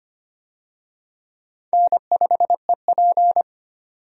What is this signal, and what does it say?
Morse code, heard as a keyed tone, about 25 wpm, 700 Hz. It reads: N5EP
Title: USA Call Signs - 4 Characters - 25wpm